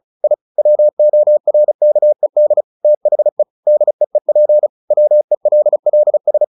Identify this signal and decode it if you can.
Morse code, heard as a keyed tone, about 34 wpm, 600 Hz. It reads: I WORKED THE DEEP WELLS
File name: the old man and the sea / the old man and the sea 1 35wpm